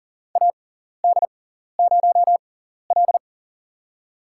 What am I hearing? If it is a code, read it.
Morse code, heard as a keyed tone, about 40 wpm, spaced as for 14 wpm, 700 Hz. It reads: AD0L
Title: USA Call Signs - 4 Characters - 6x Character Spacing - 40wpm